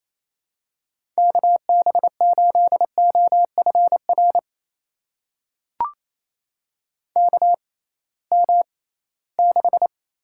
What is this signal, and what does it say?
Morse code, heard as a keyed tone, about 28 wpm, spaced as for 18 wpm, 700 Hz. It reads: KB8OFR KM6
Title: USA Call Signs - 6x Character Spacing - 28wpm